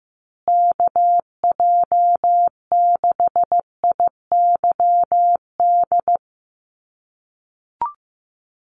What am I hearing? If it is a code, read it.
Morse code, heard as a keyed tone, about 15 wpm, 700 Hz. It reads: KJ6IYD
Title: USA Call Signs Repeated 2x 15wpm